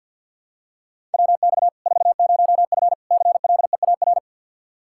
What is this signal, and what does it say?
Morse code, heard as a keyed tone, about 49 wpm, 700 Hz. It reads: WX 40F CLEAR
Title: Practice QSO - Exchange 2 - Info - 50wpm